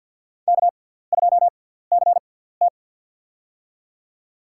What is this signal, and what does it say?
Morse code, heard as a keyed tone, about 50 wpm, spaced as for 17 wpm, 700 Hz. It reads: K2CT
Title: USA Call Signs - 4 Characters - 6x Character Spacing - 50wpm